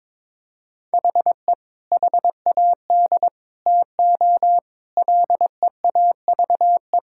Morse code, heard as a keyed tone, about 22 wpm, 700 Hz. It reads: HE HAD TO LEAVE
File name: practice / 22wpm / Sentences with 3-Word Phrases - 22wpm